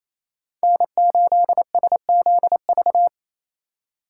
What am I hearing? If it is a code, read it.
Morse code, heard as a keyed tone, about 28 wpm, 700 Hz. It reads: N8SZV
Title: USA Call Signs - 5 Characters - 2x Character Spacing - 28wpm